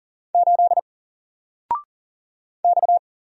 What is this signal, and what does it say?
Morse code, heard as a keyed tone, about 38 wpm, 700 Hz. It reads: 8 X
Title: Single Letter-Number Alt 40wpm